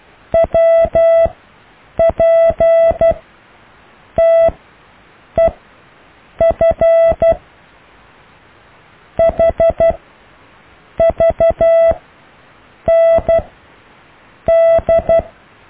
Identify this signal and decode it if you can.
Morse code, heard as a keyed tone, about 12 wpm, spaced as for 6 wpm, 660 Hz. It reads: WPTEF HVND
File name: cw / EKTPV+BFHNW